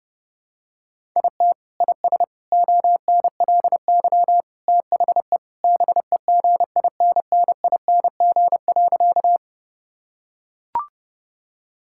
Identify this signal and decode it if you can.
Morse code, heard as a keyed tone, about 30 wpm, 700 Hz. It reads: IT IS ONLY THE BEGINNING.
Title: Sentences from Top 1000 Words - Review Part 5 - 30wpm